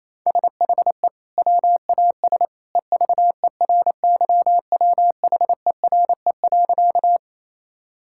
Morse code, heard as a keyed tone, about 28 wpm, 700 Hz. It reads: SHE WAS EVERYWHERE.
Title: Sentences from Top 1600 Words 28wpm